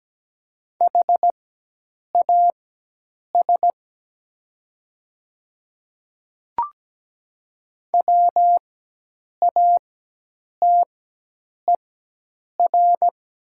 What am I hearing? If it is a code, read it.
Morse code, heard as a keyed tone, about 17 wpm, spaced as for 8 wpm, 700 Hz. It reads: HAS WATER
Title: 4x Character Spacing - Top 100 Words - 17wpm